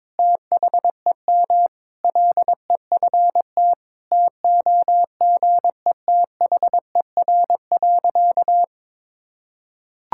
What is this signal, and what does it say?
Morse code, heard as a keyed tone, about 22 wpm, 700 Hz. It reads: THEM LEFT TOGETHER.
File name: practice / 22wpm / Sentences with 4-Word Phrases - 22wpm